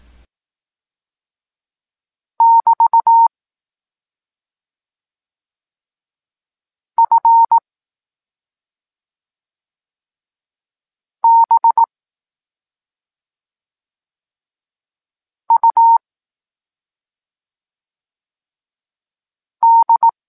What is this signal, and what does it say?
Morse code, heard as a keyed tone, about 18 wpm, 920 Hz. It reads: <BT> F B U D